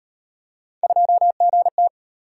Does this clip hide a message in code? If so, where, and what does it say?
Yes: Morse code from 0.8 to 1.9 s, heard as a keyed tone, about 36 wpm, 700 Hz. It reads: 2GT